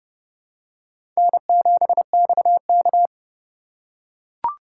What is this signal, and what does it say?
Morse code, heard as a keyed tone, about 32 wpm, 700 Hz. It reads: N7XK
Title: USA Call Signs - 4x Character Spacing - 30wpm